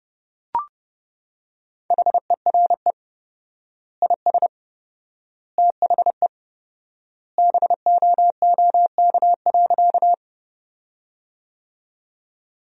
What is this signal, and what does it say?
Morse code, heard as a keyed tone, about 30 wpm, 700 Hz. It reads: HERE IS THE BOOK.
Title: 4x Word-Spacing - Sentences from Top 300 Words - 30wpm